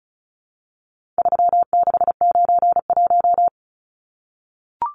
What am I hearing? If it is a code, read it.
Morse code, heard as a keyed tone, about 34 wpm, 700 Hz. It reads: 3691
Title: Numbers - 4-Digit 35wpm